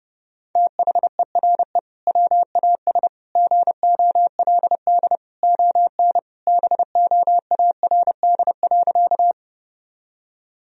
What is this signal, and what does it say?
Morse code, heard as a keyed tone, about 30 wpm, 700 Hz. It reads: THERE WAS GOLD ON BOARD.
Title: Sentences from Top 1000 Words - Review Part 4 - 30wpm